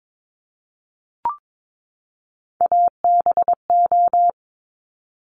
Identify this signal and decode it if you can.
Morse code, heard as a keyed tone, about 22 wpm, 690 Hz. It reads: ABO
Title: Top Three Letter Combinations in Words 22wpm